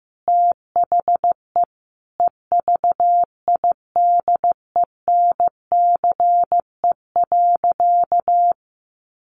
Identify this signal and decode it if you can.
Morse code, heard as a keyed tone, about 14 wpm, 700 Hz. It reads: THE EVIDENCE.
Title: Sentences from Top 1300 Words 15wpm